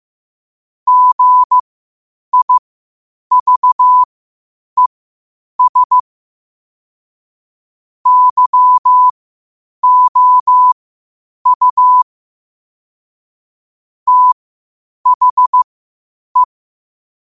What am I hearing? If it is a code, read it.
Morse code, heard as a keyed tone, about 15 wpm, spaced as for 8 wpm, 1000 Hz. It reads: GIVES YOU THE
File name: qotdfiles / 8wpm